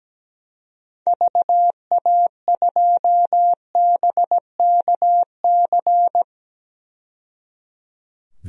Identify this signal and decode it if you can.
Morse code, heard as a keyed tone, about 17 wpm, 700 Hz. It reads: VA2BKC
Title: DXCC-100 Call Signs - 6 Characters - 17wpm